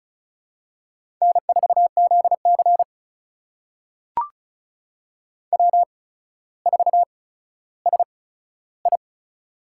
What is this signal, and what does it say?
Morse code, heard as a keyed tone, about 36 wpm, spaced as for 15 wpm, 700 Hz. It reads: N4ZC W4SI
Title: USA Call Signs - 4 Characters - 8x Character Spacing - 35wpm